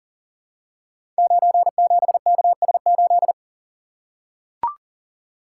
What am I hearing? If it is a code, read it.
Morse code, heard as a keyed tone, about 40 wpm, 700 Hz. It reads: 97KS8
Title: Code Groups - 5 Letter:Numbers - 40wpm